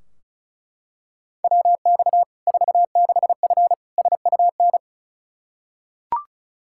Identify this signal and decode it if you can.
Morse code, heard as a keyed tone, about 34 wpm, 700 Hz. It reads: WX 46F SUN
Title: Practice QSO - Exchange 2 - Info - 35wpm